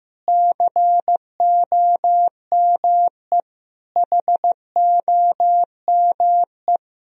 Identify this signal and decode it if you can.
Morse code, heard as a keyed tone, about 15 wpm, 700 Hz. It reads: COME HOME
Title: Sentences from Top 1000 Words - Review Part 2 - 15wpm